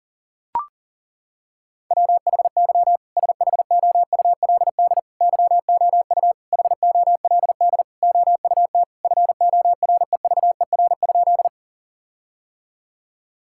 Morse code, heard as a keyed tone, about 40 wpm, 700 Hz. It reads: WHY SHOULD YOU HOLD OUT FOREVER?
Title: Sentences from Top 1600 Words 40wpm